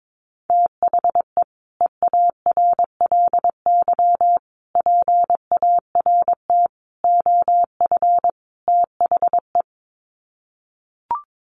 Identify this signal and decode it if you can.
Morse code, heard as a keyed tone, about 22 wpm, 700 Hz. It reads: THE EARLY PART OF THE
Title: Sets of 5 Words Encore2 22wpm